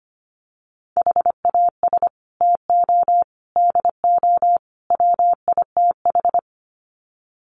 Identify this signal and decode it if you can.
Morse code, heard as a keyed tone, about 25 wpm, 700 Hz. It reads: HAS TO DO WITH
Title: Sets of 4 Words 25wpm